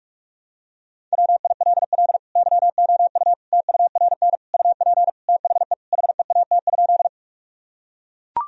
Morse code, heard as a keyed tone, about 41 wpm, 700 Hz. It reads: WILL YOU TURN UP THE HEAT?
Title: Sentences from Top 1000 Words - Review Part 3 - 45wpm